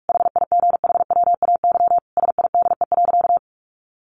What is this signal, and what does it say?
Morse code, heard as a keyed tone, about 44 wpm, 700 Hz. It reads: HIGHWAY SIDE.
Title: Sentences from Top 1400 Words 45wpm